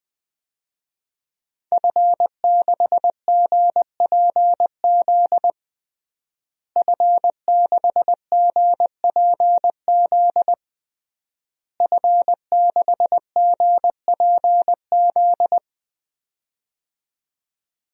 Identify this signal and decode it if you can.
Morse code, heard as a keyed tone, about 20 wpm, 700 Hz. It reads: F6GPZ F6GPZ F6GPZ
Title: DXCC-100 Call Signs - 5 Characters - Repeated 3x - 20wpm